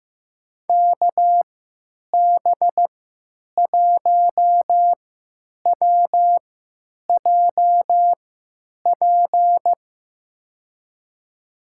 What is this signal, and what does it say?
Morse code, heard as a keyed tone, about 15 wpm, spaced as for 9 wpm, 700 Hz. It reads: KB1WJP
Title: USA Call Signs - 6 Characters - 3x Character Spacing - 15wpm